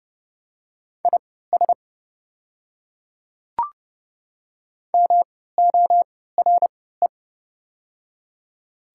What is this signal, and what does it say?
Morse code, heard as a keyed tone, about 30 wpm, spaced as for 17 wpm, 700 Hz. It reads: IS MORE
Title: 3x Character Spacing - Top 100 Words - 30wpm